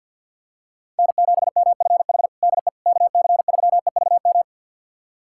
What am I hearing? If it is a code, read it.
Morse code, heard as a keyed tone, about 50 wpm, 700 Hz. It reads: N7MFH DE KC3EVM
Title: USA Callsign DE Callsign 50wpm